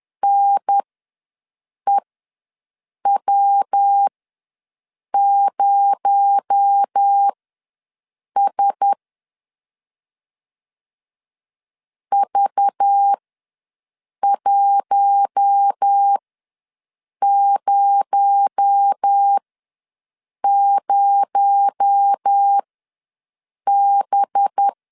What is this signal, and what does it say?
Morse code, heard as a keyed tone, about 10 wpm, spaced as for 6 wpm, 800 Hz. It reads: NEW0S V100B
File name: LEZIONI / 6 PKX0 / 6 PKX0 ws 10wpm 4220